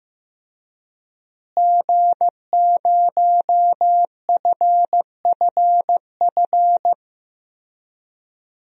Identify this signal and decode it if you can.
Morse code, heard as a keyed tone, about 15 wpm, 700 Hz. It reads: G0FFF
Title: DXCC-100 Call Signs - 5 Characters - Repeated 3x - 15wpm